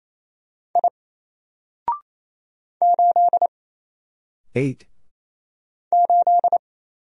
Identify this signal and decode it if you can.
Morse code, heard as a keyed tone, about 27 wpm, 700 Hz. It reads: I 8 8
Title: Single Letter-Number 28wpm